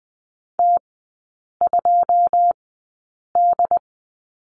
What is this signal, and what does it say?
Morse code, heard as a keyed tone, about 20 wpm, spaced as for 8 wpm, 700 Hz. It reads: T2D
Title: 093 Lesson 16e - Letter c - Three Characters 20wpm